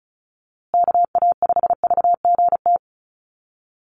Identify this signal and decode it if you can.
Morse code, heard as a keyed tone, about 34 wpm, 700 Hz. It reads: KA5VGT
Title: USA Call Signs Repeated 3x 35wpm